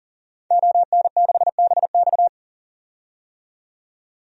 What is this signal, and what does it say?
Morse code, heard as a keyed tone, about 40 wpm, 700 Hz. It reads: ON6BX
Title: DXCC-100 Call Signs - 5 Characters - 40wpm